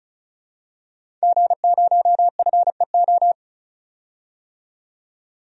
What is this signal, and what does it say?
Morse code, heard as a keyed tone, about 36 wpm, 700 Hz. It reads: G0FEO
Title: DXCC-100 Call Signs - 5 Characters - 35wpm